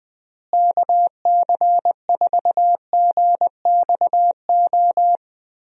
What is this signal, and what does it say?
Morse code, heard as a keyed tone, about 20 wpm, 700 Hz. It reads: KC4GXO
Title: USA Call Signs - 6 Characters - Repeated 3x - 20wpm